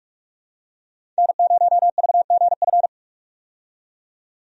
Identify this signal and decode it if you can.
Morse code, heard as a keyed tone, about 46 wpm, 700 Hz. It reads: N0VGF
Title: USA Call Signs - 5 Characters - 45wpm